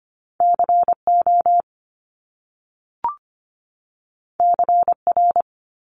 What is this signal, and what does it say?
Morse code, heard as a keyed tone, about 25 wpm, 700 Hz. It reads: CO CR
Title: Top Two Letter Combinations in Words 25wpm